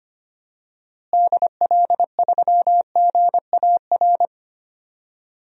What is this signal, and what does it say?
Morse code, heard as a keyed tone, about 25 wpm, 700 Hz. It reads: DL3GAR
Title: DXCC-100 Call Signs - 6 Characters - Repeated 3x - 25wpm